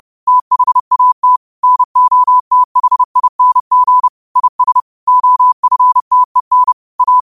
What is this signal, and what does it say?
Morse code, heard as a keyed tone, about 30 wpm, 1000 Hz. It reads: THAT NOTHING IS OFTEN A
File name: qotdfiles / 30wpm